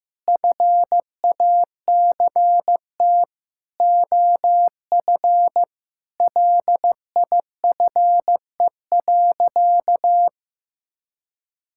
Morse code, heard as a keyed tone, about 16 wpm, 700 Hz. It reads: FACT OF LIFE.
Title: Sentences from Top 1000 Words - Review Part 5 - 15wpm